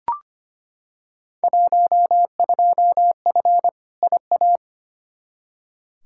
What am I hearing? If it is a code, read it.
Morse code, heard as a keyed tone, about 25 wpm, 700 Hz. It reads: 12F IA
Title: Field Day Exchanges Encore - 25wpm